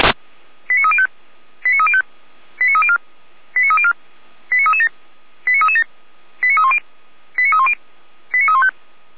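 Five-tone selective call (ZVEI), heard as quick runs of five tones, tones of about 70 ms. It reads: 89305 89305 89304 89304 89337 89337 89310 89310 89315
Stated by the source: ZVEI five-tone selective call received on an SDR.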